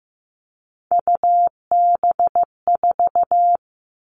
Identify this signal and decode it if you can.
Morse code, heard as a keyed tone, about 15 wpm, 700 Hz. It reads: UB4
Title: All Cut Numbers - 3-Digit 15wpm